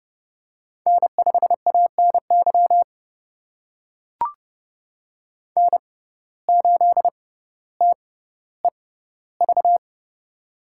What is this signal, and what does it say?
Morse code, heard as a keyed tone, about 30 wpm, spaced as for 15 wpm, 700 Hz. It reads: N5ANY N8TEV
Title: USA Call Signs - 6x Character Spacing - 30wpm